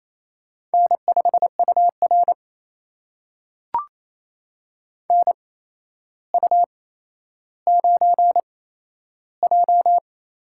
Morse code, heard as a keyed tone, about 28 wpm, spaced as for 12 wpm, 700 Hz. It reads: N5UR NU9J
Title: USA Call Signs - 4 Characters - 8x Character Spacing - 28wpm